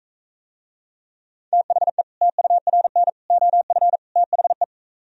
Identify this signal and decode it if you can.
Morse code, heard as a keyed tone, about 44 wpm, 700 Hz. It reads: THE TURN OF THE